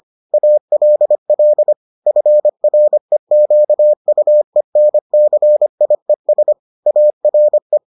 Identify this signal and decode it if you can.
Morse code, heard as a keyed tone, about 25 wpm, 600 Hz. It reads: ALL FREQUENCIES ARE